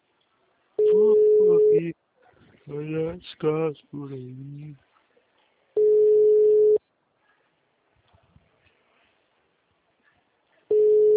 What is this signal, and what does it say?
Ringback tone (Europe, 425 Hz), with speech.